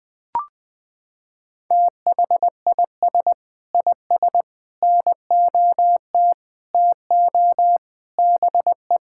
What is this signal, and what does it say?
Morse code, heard as a keyed tone, about 20 wpm, 700 Hz. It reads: THIS IS NOT TO BE
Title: Sentences with 7-Word Phrases - 20wpm